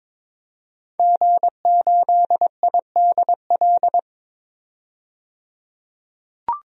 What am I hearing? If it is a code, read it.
Morse code, heard as a keyed tone, about 23 wpm, 700 Hz. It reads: G8IDL
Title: DXCC-100 Call Signs - 5 Characters - 22wpm